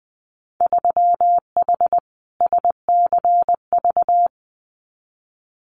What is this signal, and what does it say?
Morse code, heard as a keyed tone, about 20 wpm, 700 Hz. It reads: 3H SCV
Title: Winter Field Day Exchanges Encore 20wpm